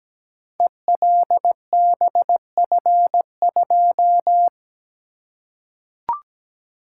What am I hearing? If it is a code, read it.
Morse code, heard as a keyed tone, about 17 wpm, 700 Hz. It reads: ELBF2
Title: Code Groups - 5 Letter:Numbers - 17wpm